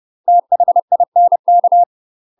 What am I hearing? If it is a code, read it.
Morse code, heard as a keyed tone, about 30 wpm, 700 Hz. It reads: THINK
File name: UK MASTER QSOs TRAINING / 30WPM / UK MASTER QSOs 0007